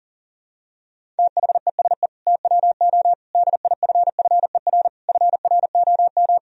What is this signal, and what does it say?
Morse code, heard as a keyed tone, about 40 wpm, 700 Hz. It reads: THESE TWO DIFFER FROM
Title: Sentences from Top 1000 Words - Review Part 1 - 40wpm